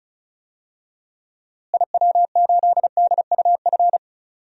Morse code, heard as a keyed tone, about 36 wpm, 700 Hz. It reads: IW8DUF
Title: DXCC-100 Call Signs - 6 Characters - Repeated 2x - 35wpm